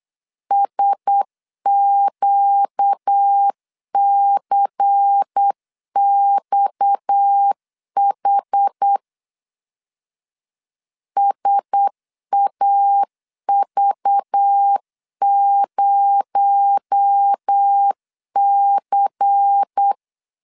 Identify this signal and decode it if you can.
Morse code, heard as a keyed tone, about 8 wpm, 800 Hz. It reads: SQCXH SAV0C
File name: CW / DET / 1 6 1500C S C 8wpm 5219